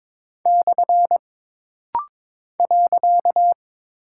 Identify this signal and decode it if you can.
Morse code, heard as a keyed tone, about 23 wpm, 700 Hz. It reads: / .
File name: practice / 22wpm / Single Punctuation-Prosign - Rapid-Fire - 22wpm